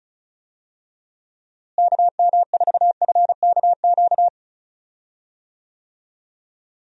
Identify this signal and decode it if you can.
Morse code, heard as a keyed tone, about 36 wpm, 700 Hz. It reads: KM4FKQ
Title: USA Call Signs - 6 Characters - Repeated 2x - 35wpm